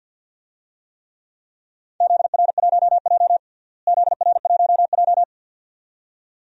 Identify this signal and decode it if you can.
Morse code, heard as a keyed tone, about 45 wpm, 700 Hz. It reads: ZR1J ZR1J
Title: DXCC-100 Call Signs - 4 Characters - Repeated 2x - 50wpm